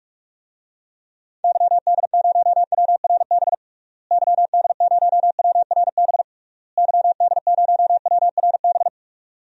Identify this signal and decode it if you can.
Morse code, heard as a keyed tone, about 46 wpm, 700 Hz. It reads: YD0WRB YD0WRB YD0WRB
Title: DXCC-100 Call Signs - 6 Characters - Repeated 3x - 45wpm